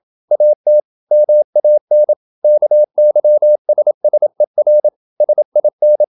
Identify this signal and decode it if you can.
Morse code, heard as a keyed tone, about 27 wpm, 600 Hz. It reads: AT MAN KYSSER SIN
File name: Ferie QRQ 27wpm